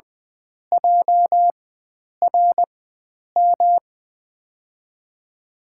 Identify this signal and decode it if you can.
Morse code, heard as a keyed tone, about 21 wpm, spaced as for 9 wpm, 700 Hz. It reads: JRM
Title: USA Call Signs - 5 Characters - 4x Character Spacing - 20wpm